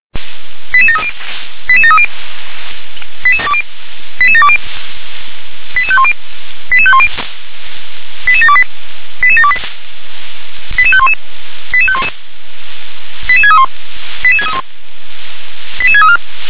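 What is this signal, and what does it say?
Five-tone selective call (ZVEI), heard as quick runs of five tones, tones of about 70 ms. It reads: 88730 88730 88720 88510 88510 88728 88728 88511 88511 88631 88631 88634
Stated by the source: ZVEI five-tone selective call received on an SDR.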